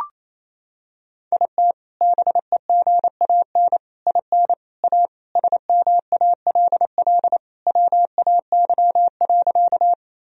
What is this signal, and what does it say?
Morse code, heard as a keyed tone, about 28 wpm, 700 Hz. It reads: IT BEGAN IN A SMALL WAY.
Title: Sentences from Top 1000 Words - Review Part 1 - 28wpm